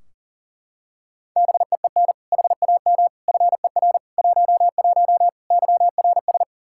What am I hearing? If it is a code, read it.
Morse code, heard as a keyed tone, about 40 wpm, 690 Hz. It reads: BEEN HAM FER 11 YRS
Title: Practice QSO - Exchange 3 - More Info - 40wpm